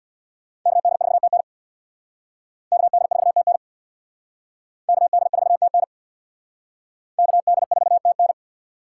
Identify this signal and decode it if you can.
Morse code, heard as a keyed tone, about 60 wpm, 700 Hz. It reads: KD4TN KD4TN KD4TN KD4TN
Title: Speed Racing - USA Call Signs 50wpm